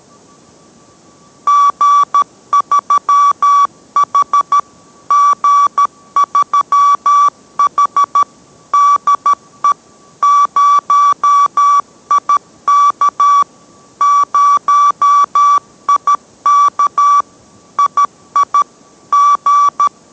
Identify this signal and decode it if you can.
Morse code, heard as a keyed tone, about 13 wpm, 1150 Hz. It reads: G3H G3H DE 0IK 0IK II G